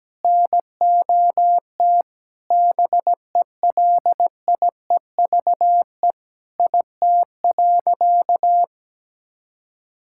Morse code, heard as a keyed tone, about 17 wpm, 700 Hz. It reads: NOT BELIEVE IT.
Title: Sentences from Top 1000 Words - Review Part 6 - 17wpm